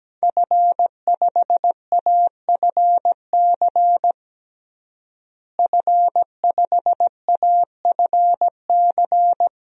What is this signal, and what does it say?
Morse code, heard as a keyed tone, about 17 wpm, 700 Hz. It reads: F5AFC F5AFC
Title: DXCC-100 Call Signs - 5 Characters - Repeated 3x - 17wpm